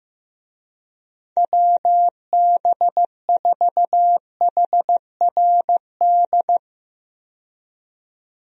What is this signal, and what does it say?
Morse code, heard as a keyed tone, about 15 wpm, 700 Hz. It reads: WB4HRD
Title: USA Call Signs - 6 Characters - 15wpm